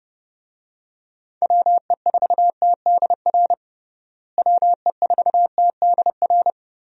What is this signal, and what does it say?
Morse code, heard as a keyed tone, about 32 wpm, 700 Hz. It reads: WE4TDR WE4TDR
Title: USA Call Signs - 6 Characters - Repeated 3x - 30wpm